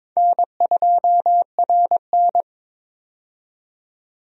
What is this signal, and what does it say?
Morse code, heard as a keyed tone, about 22 wpm, 700 Hz. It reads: N2RN
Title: USA Call Signs - 4 Characters - 22wpm